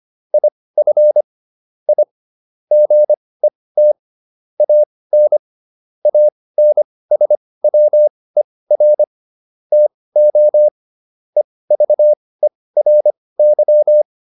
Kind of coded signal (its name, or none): Morse code